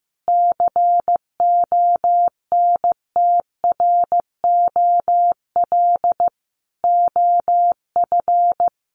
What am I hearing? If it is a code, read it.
Morse code, heard as a keyed tone, about 15 wpm, 700 Hz. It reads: CONTROL OF
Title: Sentences from Top 1200 Words 15wpm